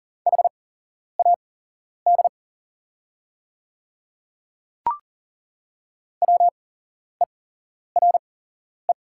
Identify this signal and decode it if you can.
Morse code, heard as a keyed tone, about 40 wpm, spaced as for 11 wpm, 700 Hz. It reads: HAD WERE
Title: 8x Character Spacing - Top 100 Words - 40wpm